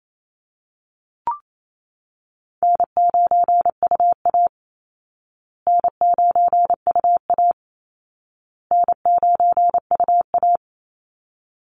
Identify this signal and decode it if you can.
Morse code, heard as a keyed tone, about 28 wpm, 700 Hz. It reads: N9UA N9UA N9UA
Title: Easy Call Signs Repeated 3x - 28wpm